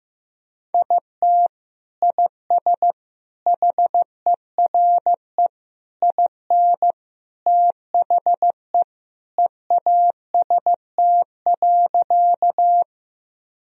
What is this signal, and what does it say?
Morse code, heard as a keyed tone, about 16 wpm, 700 Hz. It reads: IT IS HERE IN THE EAST.
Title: Sentences from Top 1000 Words - Review Part 3 - 15wpm